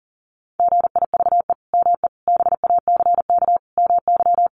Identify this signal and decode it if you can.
Morse code, heard as a keyed tone, about 40 wpm, 700 Hz. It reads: GIVE ME BACK MY
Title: Sentences from Top 800 Words 40wpm